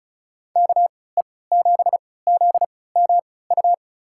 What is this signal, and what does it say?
Morse code, heard as a keyed tone, about 34 wpm, spaced as for 20 wpm, 700 Hz. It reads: KE7ZMU
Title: USA Call Signs - 6 Characters - 3x Character Spacing - 35wpm